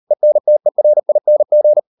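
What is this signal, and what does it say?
Morse code, heard as a keyed tone, about 38 wpm, 600 Hz. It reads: ENTERING
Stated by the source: Morse code practice at 38 WPM, sending Ken Follett's novel Never in English, from CW med Gnister.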